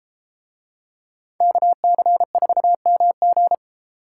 Morse code, heard as a keyed tone, about 32 wpm, 700 Hz. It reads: KC4MG